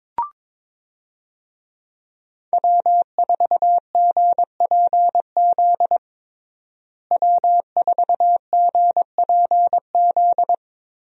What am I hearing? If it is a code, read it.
Morse code, heard as a keyed tone, about 22 wpm, 700 Hz. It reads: W4GPZ W4GPZ
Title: USA Call Signs - 5 Characters - Repeated 3x - 22wpm